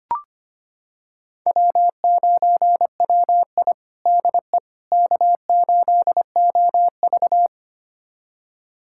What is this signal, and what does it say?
Morse code, heard as a keyed tone, about 25 wpm, 700 Hz. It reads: W9WI DE K8OV
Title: USA Callsign DE Callsign - 4 Characters - 25wpm